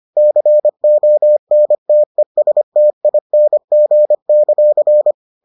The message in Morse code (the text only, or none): CONTESTING;